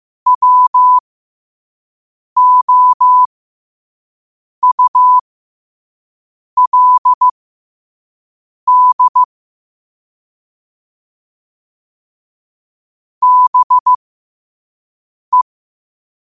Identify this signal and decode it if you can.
Morse code, heard as a keyed tone, about 15 wpm, spaced as for 5 wpm, 1000 Hz. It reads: WOULD BE
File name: 5wpm